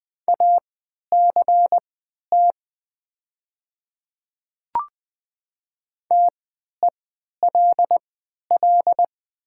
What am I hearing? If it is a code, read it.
Morse code, heard as a keyed tone, about 20 wpm, spaced as for 11 wpm, 700 Hz. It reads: ACT TELL
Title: 3x Character Spacing - Top 200 Words - 20wpm